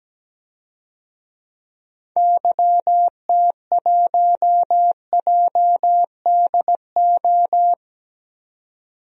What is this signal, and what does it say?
Morse code, heard as a keyed tone, about 17 wpm, 700 Hz. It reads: YT1JDO
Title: DXCC-100 Call Signs - 6 Characters - 17wpm